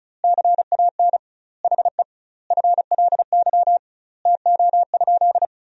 Morse code, heard as a keyed tone, about 36 wpm, 700 Hz. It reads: CAN HE FLY TO?
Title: 2x Word-Spacing - Sentences from Top 500 Words - 35wpm